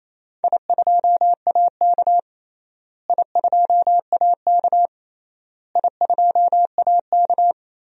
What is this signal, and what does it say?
Morse code, heard as a keyed tone, about 28 wpm, 700 Hz. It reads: I2AK I2AK I2AK
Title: DXCC-100 Call Signs - 4 Characters - Repeated 3x - 28wpm